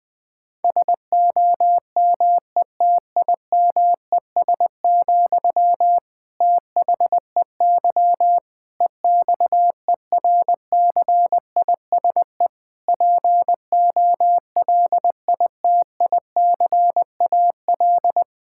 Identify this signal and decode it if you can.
Morse code, heard as a keyed tone, about 20 wpm, 700 Hz. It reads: SOMETIMES, THEY EXERCISE POLITICAL
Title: Sentences from Top 1000 Words - Review Part 6 - 20wpm